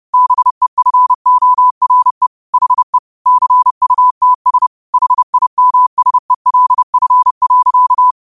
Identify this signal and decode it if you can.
Morse code, heard as a keyed tone, about 30 wpm, 1000 Hz. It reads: BEFORE HE CUTS HIMSELF.
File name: 30wpm